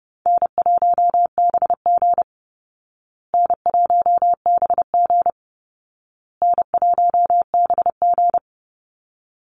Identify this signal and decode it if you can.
Morse code, heard as a keyed tone, about 30 wpm, 700 Hz. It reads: N1BG N1BG N1BG
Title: Easy Call Signs Repeated 3x - 30wpm